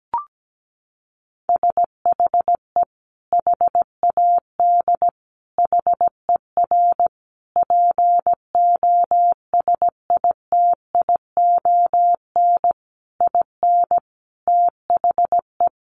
Morse code, heard as a keyed tone, about 17 wpm, 700 Hz. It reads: SHE HAD HER POSITION IN THE
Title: Sentences from Top 900 Words 17wpm